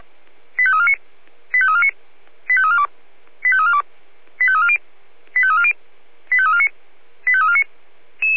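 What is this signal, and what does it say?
Five-tone selective call (ZVEI), heard as quick runs of five tones, tones of about 70 ms. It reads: 85349 85349 85342 85342 85340 85340 85349 85349
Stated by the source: ZVEI five-tone selective call received by SDR.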